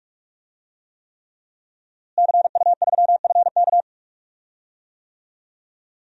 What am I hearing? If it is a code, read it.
Morse code, heard as a keyed tone, about 46 wpm, 700 Hz. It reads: CU3FK
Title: DXCC-100 Call Signs - 5 Characters - Repeated 3x - 45wpm